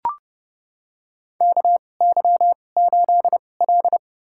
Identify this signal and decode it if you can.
Morse code, heard as a keyed tone, about 32 wpm, spaced as for 22 wpm, 700 Hz. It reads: KY8L
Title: USA Call Signs - 4 Characters - 2x Character Spacing - 30wpm